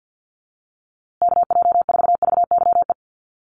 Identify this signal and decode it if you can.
Morse code, heard as a keyed tone, about 50 wpm, 700 Hz. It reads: KJ4VYE
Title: USA Call Signs Repeated 2x 50wpm